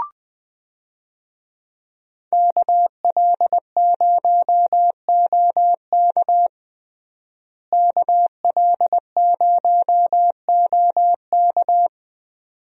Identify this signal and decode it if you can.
Morse code, heard as a keyed tone, about 21 wpm, 700 Hz. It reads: KL0OK KL0OK
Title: USA Call Signs - 5 Characters - Repeated 2x - 20wpm